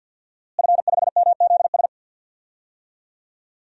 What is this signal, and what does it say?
Morse code, heard as a keyed tone, about 51 wpm, 700 Hz. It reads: F5MZS
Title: DXCC-100 Call Signs - 5 Characters - Repeated 3x - 50wpm